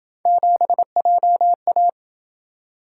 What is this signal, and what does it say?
Morse code, heard as a keyed tone, about 27 wpm, 700 Hz. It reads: 7JA